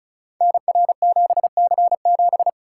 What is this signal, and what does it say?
Morse code, heard as a keyed tone, about 36 wpm, 700 Hz. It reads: NR7C7
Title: Code Groups - 5 Letter:Numbers - 35wpm